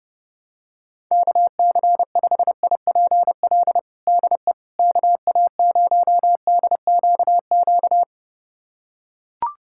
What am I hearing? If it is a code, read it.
Morse code, heard as a keyed tone, about 30 wpm, 700 Hz. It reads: KC5IPL DE KA0DQQ
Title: USA Callsign DE Callsign - 6 Characters - 30wpm